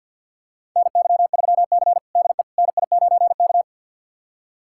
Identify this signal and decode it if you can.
Morse code, heard as a keyed tone, about 52 wpm, 700 Hz. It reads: NY3C DE NI9K
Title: USA Callsign DE Callsign - 4 Characters - 50wpm